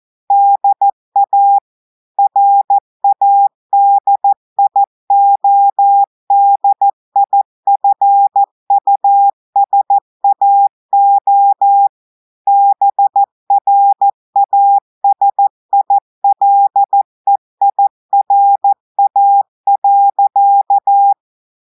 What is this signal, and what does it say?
Morse code, heard as a keyed tone, about 14 wpm, 800 Hz. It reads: DA RADIODIFUSAO BRASILEIRA.
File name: audios / portugais / portugais texte en clair 1 vitesse 14 mots minutes